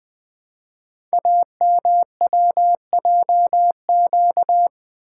Morse code, heard as a keyed tone, about 20 wpm, 700 Hz. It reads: AMWJQ
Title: Code Groups - 5 Letters - 20wpm